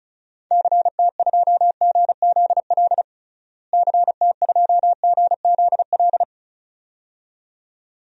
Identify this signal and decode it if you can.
Morse code, heard as a keyed tone, about 34 wpm, 700 Hz. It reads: CT2GZL CT2GZL
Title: DXCC-100 Call Signs - 6 Characters - Repeated 2x - 35wpm